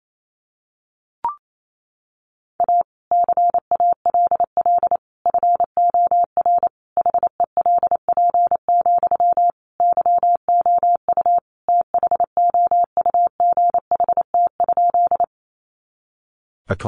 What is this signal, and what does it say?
Morse code, heard as a keyed tone, about 28 wpm, 700 Hz. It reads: A CALL FOR HELP, YOU THOUGHT?
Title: Sentences from Top 300 Words 28wpm